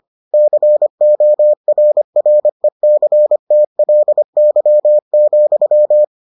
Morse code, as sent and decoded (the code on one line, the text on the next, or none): -.-. --- .-. .-. -.-. - .-.. -.-- --..--
CORRECTLY,